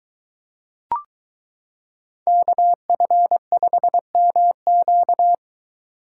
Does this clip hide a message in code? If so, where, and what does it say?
Yes: Morse code from 2.3 to 5.3 s, heard as a keyed tone, about 22 wpm, 700 Hz. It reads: KF5MQ